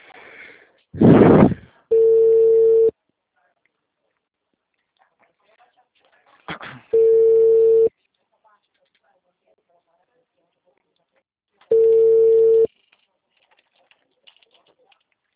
Ringback tone (China, 450 Hz).